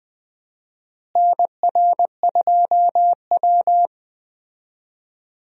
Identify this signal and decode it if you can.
Morse code, heard as a keyed tone, about 20 wpm, 700 Hz. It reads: NR2W
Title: USA Call Signs - 4 Characters - Repeated 3x - 20wpm